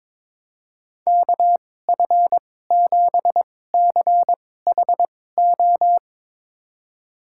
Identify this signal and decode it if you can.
Morse code, heard as a keyed tone, about 22 wpm, spaced as for 16 wpm, 700 Hz. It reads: KF7CHO
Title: USA Call Signs - 2x Character Spacing - 22wpm